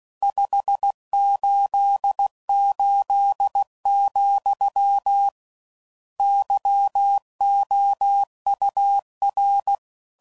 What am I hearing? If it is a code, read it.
Morse code, heard as a keyed tone, about 16 wpm, 800 Hz. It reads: 588, YOUR
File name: CW Code Groups @15WPM / RSTs / Your RSTs